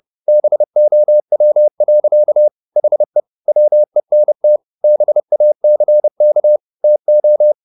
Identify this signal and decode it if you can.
Morse code, heard as a keyed tone, about 30 wpm, 600 Hz. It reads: BOW. HE WENT BACK TO
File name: the old man and the sea / the old man and the sea 1 30wpm